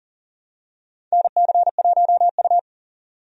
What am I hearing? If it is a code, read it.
Morse code, heard as a keyed tone, about 40 wpm, 700 Hz. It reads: NC1U
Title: USA Call Signs - 4 Characters - 40wpm